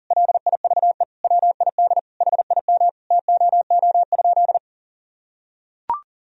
Morse code, heard as a keyed tone, about 40 wpm, 700 Hz. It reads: LIVE WID HIM TOO?
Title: Sentences from Top 500 Words and Abbreviations 40wpm